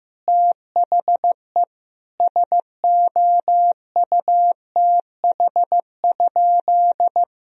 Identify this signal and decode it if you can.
Morse code, heard as a keyed tone, about 15 wpm, 700 Hz. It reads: THE SOUTH?
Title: Sentences from Top 1000 Words - Review Part 2 - 15wpm